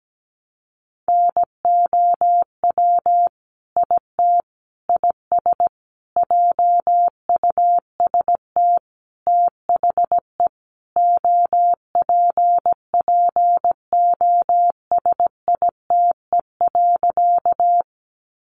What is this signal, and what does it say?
Morse code, heard as a keyed tone, about 17 wpm, 700 Hz. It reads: NOW IT IS JUST THE OPPOSITE.
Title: Sentences from Top 1300 Words 17wpm